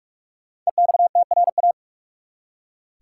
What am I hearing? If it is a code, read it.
Morse code, heard as a keyed tone, about 46 wpm, 700 Hz. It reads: EXTRA
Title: Five Letter Words - 45wpm